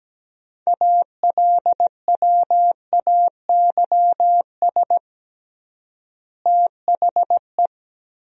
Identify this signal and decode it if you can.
Morse code, heard as a keyed tone, about 17 wpm, 700 Hz. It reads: ALWAYS THE
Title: 3x Word-Spacing - Sentences from Top 300 Words - 17wpm